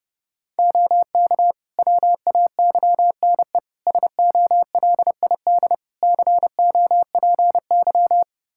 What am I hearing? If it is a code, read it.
Morse code, heard as a keyed tone, about 32 wpm, 700 Hz. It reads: OK WAYNE SOLID COPY
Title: Practice QSO - Exchange 3 - More Info - 30wpm